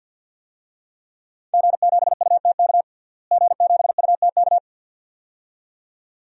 Morse code, heard as a keyed tone, about 52 wpm, 700 Hz. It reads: G7UTK G7UTK
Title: DXCC-100 Call Signs - 5 Characters - Repeated 2x - 50wpm